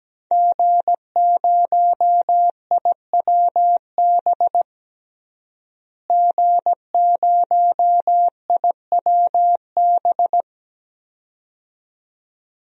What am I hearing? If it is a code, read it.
Morse code, heard as a keyed tone, about 17 wpm, 700 Hz. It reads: G0IWB G0IWB
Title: DXCC-100 Call Signs - 5 Characters - Repeated 2x - 17wpm